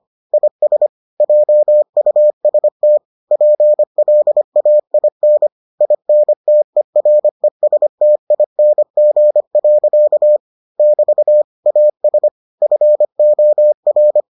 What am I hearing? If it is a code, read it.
Morse code, heard as a keyed tone, about 25 wpm, 600 Hz. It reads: IS JUST PLAIN INTERESTING. <BT> AS FOR